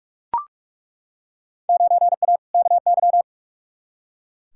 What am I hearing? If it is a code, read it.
Morse code, heard as a keyed tone, about 43 wpm, 700 Hz. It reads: 9A KY
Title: Field Day Exchanges Encore - 45wpm